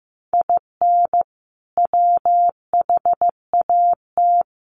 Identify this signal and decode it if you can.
Morse code, heard as a keyed tone, about 15 wpm, 700 Hz. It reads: IN WHAT
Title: Sentences from Top 200 Words 15wpm